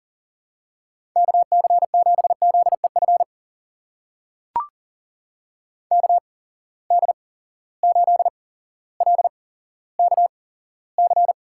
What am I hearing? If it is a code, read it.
Morse code, heard as a keyed tone, about 39 wpm, spaced as for 17 wpm, 700 Hz. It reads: KC7ZEF KD8LKC
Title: USA Call Signs - 6 Characters - 8x Character Spacing - 40wpm